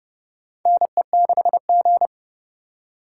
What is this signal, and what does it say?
Morse code, heard as a keyed tone, about 29 wpm, 700 Hz. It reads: NE6G
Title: USA Call Signs - 4 Characters - 6x Character Spacing - 30wpm